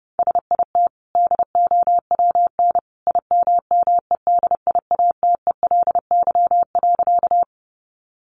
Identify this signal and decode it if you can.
Morse code, heard as a keyed tone, about 30 wpm, 700 Hz. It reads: SIT DOWN IMMEDIATELY.
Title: Sentences from Top 1100 Words 30wpm